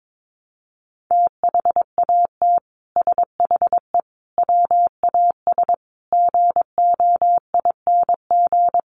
Morse code, heard as a keyed tone, about 22 wpm, 700 Hz. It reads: THAT SHE WAS GOING
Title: Sets of 5 Words Encore2 22wpm